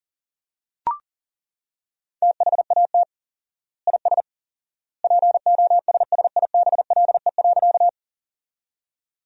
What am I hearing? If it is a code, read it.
Morse code, heard as a keyed tone, about 37 wpm, 700 Hz. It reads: THAT IS POSSIBLE.
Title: 4x Word-Spacing - Sentences from Top 500 Words - 40wpm